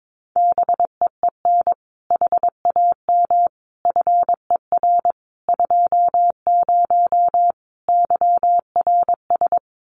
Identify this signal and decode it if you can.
Morse code, heard as a keyed tone, about 22 wpm, 700 Hz. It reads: BEEN HAM FER 20 YRS
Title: Common QSO Phrases 22wpm